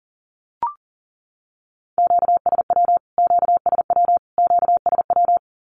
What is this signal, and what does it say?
Morse code, heard as a keyed tone, about 40 wpm, 700 Hz. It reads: QSW QSW QSW
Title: Q-Codes Repeated 3x 40wpm